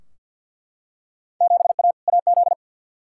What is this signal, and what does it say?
Morse code, heard as a keyed tone, about 50 wpm, 700 Hz. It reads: 7A AZ
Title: Field Day Exchanges - 50wpm